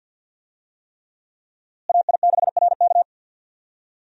Morse code, heard as a keyed tone, about 50 wpm, 700 Hz. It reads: AI6RK
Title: USA Call Signs - 5 Characters - Repeated 3x - 50wpm